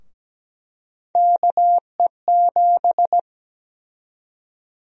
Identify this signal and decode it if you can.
Morse code, heard as a keyed tone, about 17 wpm, 700 Hz. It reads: KE7
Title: USA Call Sign Prefixes - 17wpm